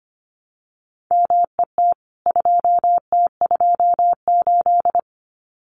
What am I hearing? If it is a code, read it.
Morse code, heard as a keyed tone, about 25 wpm, 700 Hz. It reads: MET 2T28
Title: CWT Exchange 25wpm